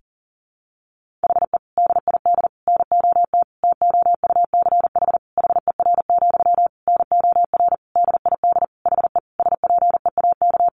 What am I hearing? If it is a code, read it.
Morse code, heard as a keyed tone, about 40 wpm, 700 Hz. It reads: HE DID NOT TOUCH HER, NOR DID HE SPEAK
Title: Sentences from Top 1500 Words 40wpm